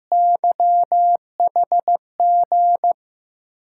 Morse code, heard as a keyed tone, about 15 wpm, 700 Hz. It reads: YHG
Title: USA Call Signs - 5 Characters - 8x Character Spacing - 15wpm